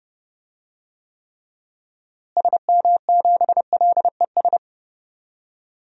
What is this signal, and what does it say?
Morse code, heard as a keyed tone, about 32 wpm, 700 Hz. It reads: SM7LES
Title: DXCC-100 Call Signs - 6 Characters - Repeated 2x - 30wpm